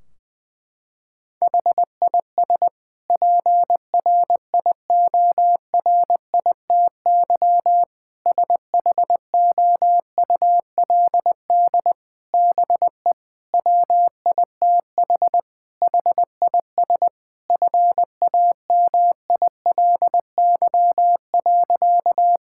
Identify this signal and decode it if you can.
Morse code, heard as a keyed tone, about 20 wpm, 700 Hz. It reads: HIS PRIORITY SHOULD BE WITH HIS FAMILY.
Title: Sentences from Top 1600 Words 20wpm